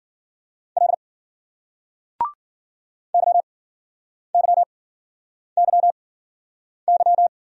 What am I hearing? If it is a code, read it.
Morse code, heard as a keyed tone, about 56 wpm, spaced as for 9 wpm, 700 Hz. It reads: L YYYY